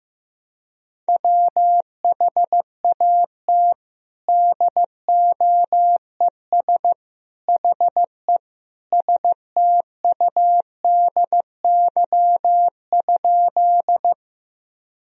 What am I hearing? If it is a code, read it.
Morse code, heard as a keyed tone, about 16 wpm, 700 Hz. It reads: WHAT DOES HE STUDY?
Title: Sentences from Top 1000 Words - Review Part 1 - 15wpm